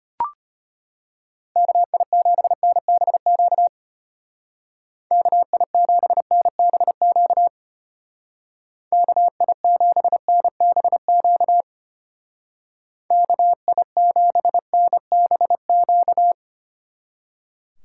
Morse code, heard as a keyed tone, about 31 wpm, 700 Hz. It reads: KI7NBQ KI7NBQ KI7NBQ KI7NBQ
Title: Speed Racing - USA 2x3 Call Signs 25wpm